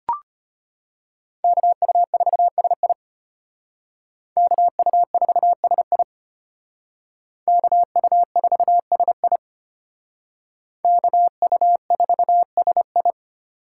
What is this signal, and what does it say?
Morse code, heard as a keyed tone, about 32 wpm, 700 Hz. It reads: KU4SI KU4SI KU4SI KU4SI